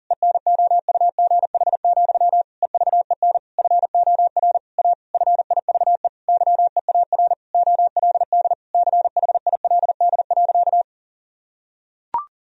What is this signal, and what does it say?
Morse code, heard as a keyed tone, about 40 wpm, 700 Hz. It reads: ENOUGH, EVEN FOR A FIVE YEAR OLD CHILD.
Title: Sentences from Top 1000 Words - Review Part 2 - 40wpm